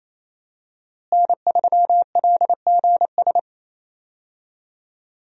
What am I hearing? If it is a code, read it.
Morse code, heard as a keyed tone, about 28 wpm, 700 Hz. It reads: N3LGS
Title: USA Call Signs - 5 Characters - Repeated 3x - 28wpm